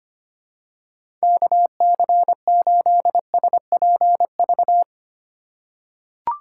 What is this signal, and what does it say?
Morse code, heard as a keyed tone, about 25 wpm, 700 Hz. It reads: KC8SPV
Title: USA Call Signs - 6 Characters - 3x Character Spacing - 25wpm